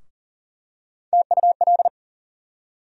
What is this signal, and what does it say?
Morse code, heard as a keyed tone, about 40 wpm, 700 Hz. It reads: TUL
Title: USA Call Sign Postfixes - 3 Letters 40wpm